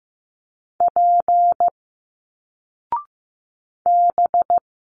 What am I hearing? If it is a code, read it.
Morse code, heard as a keyed tone, about 15 wpm, 700 Hz. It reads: P B
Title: Single Letters 15wpm